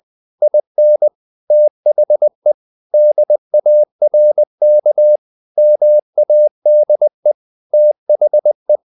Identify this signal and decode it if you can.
Morse code, heard as a keyed tone, about 20 wpm, 600 Hz. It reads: IN THE DARK MADE THE
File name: the old man and the sea / the old man and the sea part 1 20wpm